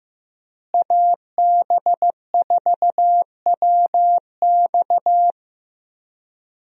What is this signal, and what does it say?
Morse code, heard as a keyed tone, about 15 wpm, 700 Hz. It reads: AB4WX
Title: USA Call Signs - 5 Characters - Repeated 3x - 15wpm